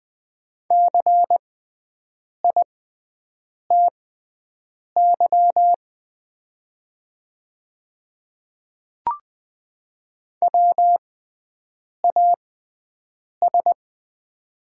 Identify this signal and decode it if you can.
Morse code, heard as a keyed tone, about 20 wpm, spaced as for 7 wpm, 700 Hz. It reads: CITY WAS
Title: 6x Character Spacing - Top 300 Words - 20wpm